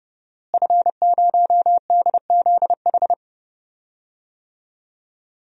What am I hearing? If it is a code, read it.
Morse code, heard as a keyed tone, about 30 wpm, 700 Hz. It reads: F0DZH
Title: DXCC-100 Call Signs - 5 Characters - Repeated 2x - 30wpm